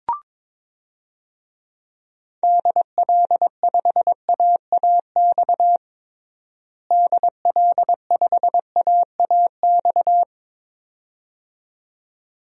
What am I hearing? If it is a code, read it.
Morse code, heard as a keyed tone, about 22 wpm, 700 Hz. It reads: DL5AAX DL5AAX
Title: DXCC-100 Call Signs - 6 Characters - Repeated 2x - 22wpm